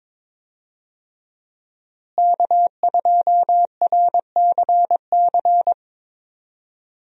Morse code, heard as a keyed tone, about 22 wpm, 700 Hz. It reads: K2RCC
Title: USA Call Signs - 5 Characters - Repeated 2x - 22wpm